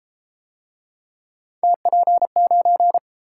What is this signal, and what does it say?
Morse code, heard as a keyed tone, about 31 wpm, 700 Hz. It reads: TP9